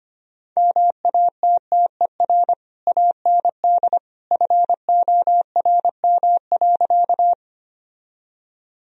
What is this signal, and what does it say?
Morse code, heard as a keyed tone, about 25 wpm, 700 Hz. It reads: MATTER AND FORM.
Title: Sentences from Top 1000 Words - Review Part 1 - 25wpm